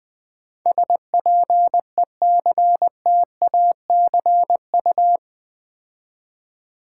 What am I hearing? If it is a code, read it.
Morse code, heard as a keyed tone, about 20 wpm, 700 Hz. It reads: SPECTACU
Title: Build A Word - 4 Letters At A Time - 11-Letter Words - 20wpm